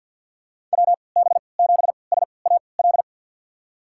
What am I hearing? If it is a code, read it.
Morse code, heard as a keyed tone, about 51 wpm, spaced as for 28 wpm, 700 Hz. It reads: WB7SAL
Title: USA Call Signs - 6 Characters - 3x Character Spacing - 50wpm